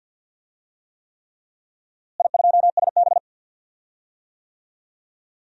Morse code, heard as a keyed tone, about 52 wpm, 700 Hz. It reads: I2SB
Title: DXCC-100 Call Signs - 4 Characters - Repeated 3x - 50wpm